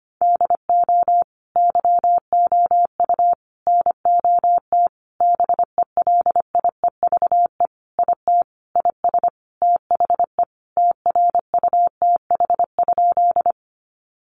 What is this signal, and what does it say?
Morse code, heard as a keyed tone, about 25 wpm, 700 Hz. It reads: DO YOU NOT BELIEVE IT IS THE TRUTH?
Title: Sentences from Top 1000 Words 25wpm